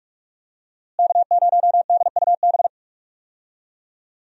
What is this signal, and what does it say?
Morse code, heard as a keyed tone, about 46 wpm, 700 Hz. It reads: K0DUB
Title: USA Call Signs - 5 Characters - Repeated 2x - 45wpm